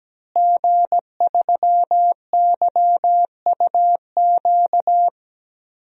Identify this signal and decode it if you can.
Morse code, heard as a keyed tone, about 17 wpm, 700 Hz. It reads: G3YUQ
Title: DXCC-100 Call Signs - 5 Characters - Repeated 3x - 17wpm